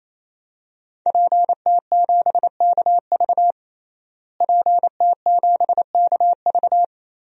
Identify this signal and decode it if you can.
Morse code, heard as a keyed tone, about 28 wpm, 700 Hz. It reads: PT7KV PT7KV
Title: DXCC-100 Call Signs - 5 Characters - Repeated 3x - 28wpm